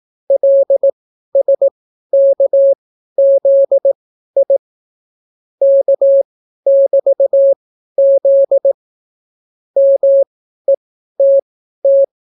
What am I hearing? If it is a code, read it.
Morse code, heard as a keyed tone, about 18 wpm, spaced as for 12 wpm, 550 Hz. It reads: LSKZI K<BT>Z METT